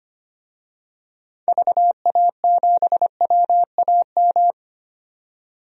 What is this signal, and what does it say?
Morse code, heard as a keyed tone, about 25 wpm, 700 Hz. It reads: VA7WAM
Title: DXCC-100 Call Signs - 6 Characters - 25wpm